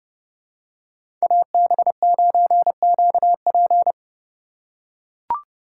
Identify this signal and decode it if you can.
Morse code, heard as a keyed tone, about 31 wpm, 700 Hz. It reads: AB9QP
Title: USA Call Signs - 5 Characters - 3x Character Spacing - 30wpm